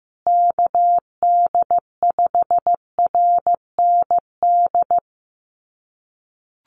Morse code, heard as a keyed tone, about 15 wpm, 700 Hz. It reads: KD5RND
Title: USA Call Signs Repeated 2x 15wpm